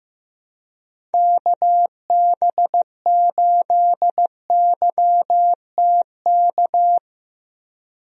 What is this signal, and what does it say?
Morse code, heard as a keyed tone, about 15 wpm, 700 Hz. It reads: KB8YTK
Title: USA Call Signs - 6 Characters - Repeated 2x - 15wpm